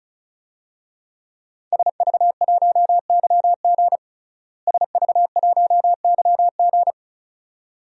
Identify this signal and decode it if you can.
Morse code, heard as a keyed tone, about 34 wpm, 700 Hz. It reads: SV1YG SV1YG
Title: DXCC-100 Call Signs - 5 Characters - Repeated 2x - 35wpm